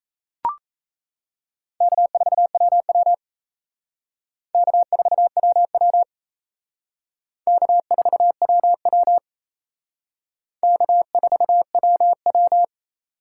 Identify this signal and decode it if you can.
Morse code, heard as a keyed tone, about 35 wpm, 700 Hz. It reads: K4WW K4WW K4WW K4WW
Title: Speed Racing - USA 1x2 Call Signs 28wpm